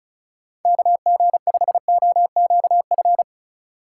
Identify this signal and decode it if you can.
Morse code, heard as a keyed tone, about 36 wpm, 700 Hz. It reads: KG5OQF
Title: USA Call Signs - 6 Characters - 2x Character Spacing - 35wpm